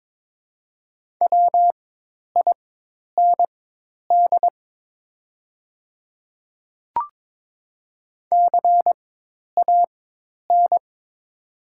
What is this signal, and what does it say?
Morse code, heard as a keyed tone, about 22 wpm, spaced as for 10 wpm, 700 Hz. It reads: WIND CAN
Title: 4x Character Spacing - Top 500 Words - 22wpm